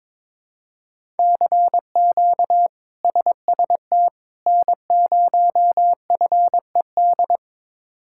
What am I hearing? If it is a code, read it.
Morse code, heard as a keyed tone, about 22 wpm, 700 Hz. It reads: CQ SST N0FED
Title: K1USN SST QSO Practice - 22wpm